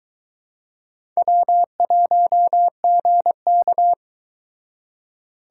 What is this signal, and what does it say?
Morse code, heard as a keyed tone, about 22 wpm, 700 Hz. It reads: W1GK